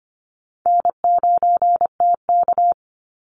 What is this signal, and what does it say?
Morse code, heard as a keyed tone, about 25 wpm, 700 Hz. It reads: N9TK
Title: Easy Call Signs Repeated 2x - 25wpm